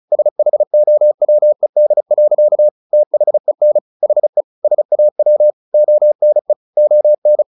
Morse code, heard as a keyed tone, about 34 wpm, 600 Hz. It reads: SHOWED. THEN HE SAW ONE ON
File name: the old man and the sea 2 35wpm